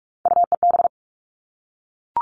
Morse code, heard as a keyed tone, about 44 wpm, 700 Hz. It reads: UEB
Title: All Cut Numbers - 3-Digit 45wpm